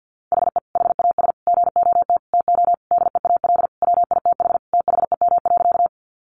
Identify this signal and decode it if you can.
Morse code, heard as a keyed tone, about 50 wpm, 700 Hz. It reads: HE HAS GOT TO DEAL WITH THEM.
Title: Sentences from Top 800 Words 50wpm